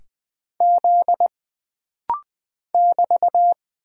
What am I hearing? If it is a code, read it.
Morse code, heard as a keyed tone, about 20 wpm, 700 Hz. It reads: Z <BT>
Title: Single Character (Letter:Number:Punctuation:Prosign) - Rapid-Fire - 20wpm